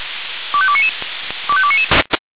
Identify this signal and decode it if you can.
Five-tone selective call (ZVEI), heard as quick runs of five tones, tones of about 70 ms. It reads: 25299 25299